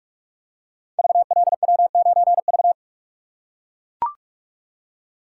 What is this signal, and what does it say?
Morse code, heard as a keyed tone, about 44 wpm, 700 Hz. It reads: VLW9V
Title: Code Groups - 5 Letter:Numbers - 45wpm